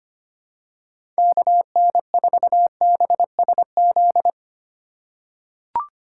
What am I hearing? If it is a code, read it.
Morse code, heard as a keyed tone, about 25 wpm, 700 Hz. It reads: KN4BSZ
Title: USA Call Signs - 6 Characters - 2x Character Spacing - 25wpm